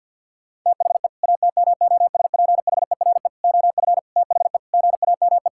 Morse code, heard as a keyed tone, about 50 wpm, 700 Hz. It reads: THE ATMOSPHERE OF THE GAME
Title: Sentences from Top 1600 Words 50wpm